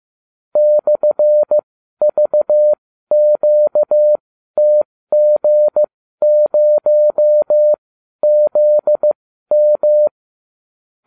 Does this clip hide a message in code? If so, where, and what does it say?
Yes: Morse code from 0.6 to 10.1 s, heard as a keyed tone, about 15 wpm, spaced as for 12 wpm, 600 Hz. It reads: /VQTG0ZM